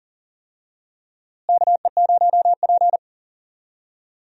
Morse code, heard as a keyed tone, about 39 wpm, 700 Hz. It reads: KE0P
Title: USA Call Signs - 4 Characters - Repeated 2x - 40wpm